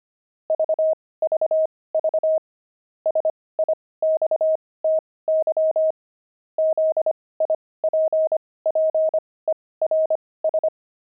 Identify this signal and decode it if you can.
Morse code, heard as a keyed tone, about 25 wpm, spaced as for 18 wpm, 630 Hz. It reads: VVV SIXTY ZIPPERS
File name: pangram-fs-18-cs-25-sixty